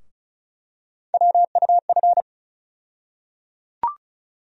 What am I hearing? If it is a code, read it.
Morse code, heard as a keyed tone, about 34 wpm, 700 Hz. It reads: WUF
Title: USA Call Sign Postfixes - 3 Letters 35wpm